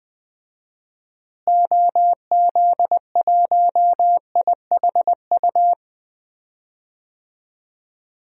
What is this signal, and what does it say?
Morse code, heard as a keyed tone, about 20 wpm, 700 Hz. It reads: OZ1IHU
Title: DXCC-100 Call Signs - 6 Characters - Repeated 2x - 20wpm